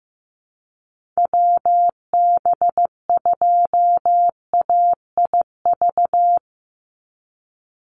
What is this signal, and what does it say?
Morse code, heard as a keyed tone, about 15 wpm, 700 Hz. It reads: WB2AIV
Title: Call Signs Repeated 2x - 15wpm